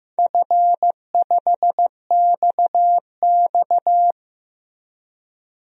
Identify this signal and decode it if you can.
Morse code, heard as a keyed tone, about 15 wpm, 700 Hz. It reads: F5XX
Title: DXCC-100 Call Signs - 4 Characters - Repeated 2x - 15wpm